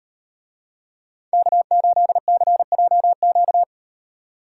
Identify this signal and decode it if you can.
Morse code, heard as a keyed tone, about 38 wpm, 700 Hz. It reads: K8CJQ